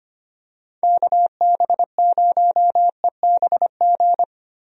Morse code, heard as a keyed tone, about 25 wpm, 700 Hz. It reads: KB0EBG
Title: USA Call Signs - 6 Characters - Repeated 2x - 25wpm